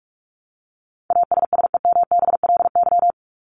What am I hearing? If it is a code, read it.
Morse code, heard as a keyed tone, about 44 wpm, 700 Hz. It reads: ASSEMBLY
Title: LY Suffix - Top 100 Words with Base Words 45wpm